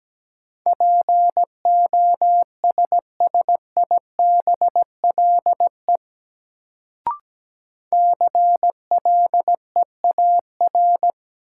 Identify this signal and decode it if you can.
Morse code, heard as a keyed tone, about 17 wpm, 700 Hz. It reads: POSSIBLE CLEAR
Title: Sentences from Top 500 Words - New Word Drill - 17wpm